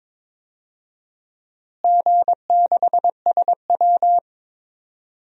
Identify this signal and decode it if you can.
Morse code, heard as a keyed tone, about 22 wpm, 700 Hz. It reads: G6SW
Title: DXCC-100 Call Signs - 4 Characters - Repeated 2x - 22wpm